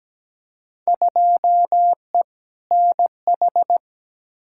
Morse code, heard as a keyed tone, about 17 wpm, 700 Hz. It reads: 2E NH
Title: Field Day Exchanges - 17wpm